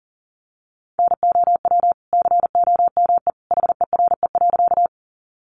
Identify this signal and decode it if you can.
Morse code, heard as a keyed tone, about 40 wpm, 700 Hz. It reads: NOW COME HERE.
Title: Sentences from Top 200 Words 40wpm